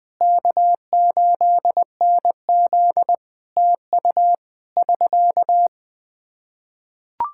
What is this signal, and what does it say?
Morse code, heard as a keyed tone, about 20 wpm, 700 Hz. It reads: K8NZ TU <SK>
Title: Practice QSO - Exchange 4 - Ending - 20wpm